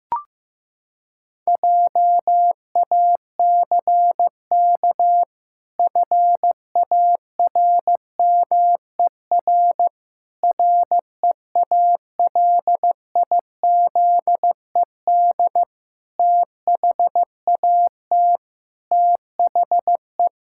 Morse code, heard as a keyed tone, about 15 wpm, 700 Hz. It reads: JACK FARMER REALIZED THAT THE
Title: Pangram Challenge v2 - 15wpm